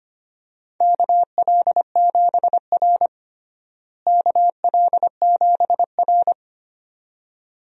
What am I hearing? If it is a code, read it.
Morse code, heard as a keyed tone, about 25 wpm, 700 Hz. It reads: KL7R KL7R
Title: DXCC-100 Call Signs - 4 Characters - Repeated 2x - 25wpm